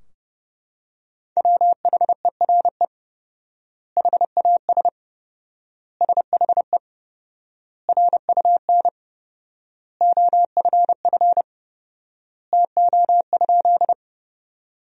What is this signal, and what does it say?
Morse code, heard as a keyed tone, about 29 wpm, 700 Hz. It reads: WHERE HAS SHE RUN OFF TO?
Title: 4x Word-Spacing - Sentences from Top 300 Words - 30wpm